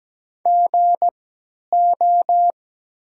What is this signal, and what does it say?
Morse code, heard as a keyed tone, about 17 wpm, spaced as for 10 wpm, 690 Hz. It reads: GO
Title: 3x Character Spacing - Top 300 Words - 17wpm